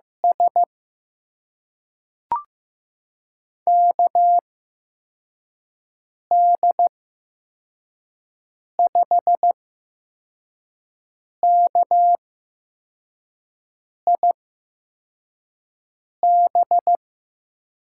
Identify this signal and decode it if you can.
Morse code, heard as a keyed tone, about 16 wpm, spaced as for 4 wpm, 700 Hz. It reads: S KD5KIB
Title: USA Call Signs - 6 Characters - 8x Character Spacing - 15wpm